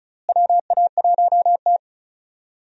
Morse code, heard as a keyed tone, about 36 wpm, 700 Hz. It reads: WA1T
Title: USA Call Signs - 4 Characters - Repeated 3x - 35wpm